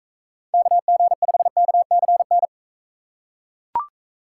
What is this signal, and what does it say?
Morse code, heard as a keyed tone, about 37 wpm, 700 Hz. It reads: KG5KCN